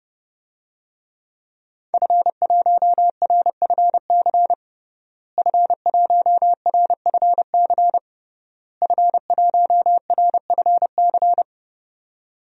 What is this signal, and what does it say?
Morse code, heard as a keyed tone, about 32 wpm, 700 Hz. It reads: F1RFC F1RFC F1RFC
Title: DXCC-100 Call Signs - 5 Characters - Repeated 3x - 30wpm